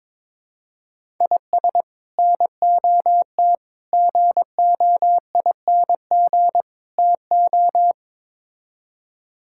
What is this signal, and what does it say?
Morse code, heard as a keyed tone, about 21 wpm, 700 Hz. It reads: IS NOT GOING TO
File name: practice / 22wpm / Build A Phrase - 5-Word Phrases - 22wpm